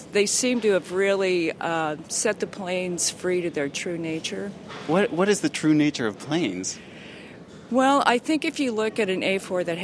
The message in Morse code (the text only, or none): none